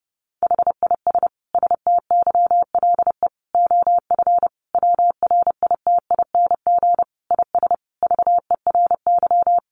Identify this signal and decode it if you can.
Morse code, heard as a keyed tone, about 30 wpm, 700 Hz. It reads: HIS STYLE OF WRITING IS VERY
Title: Sentences from Top 800 Words 30wpm